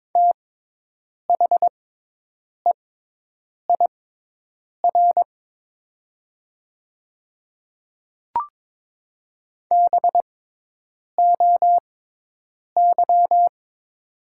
Morse code, heard as a keyed tone, about 22 wpm, spaced as for 8 wpm, 700 Hz. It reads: THEIR BOY
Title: 6x Character Spacing - Top 200 Words - 22wpm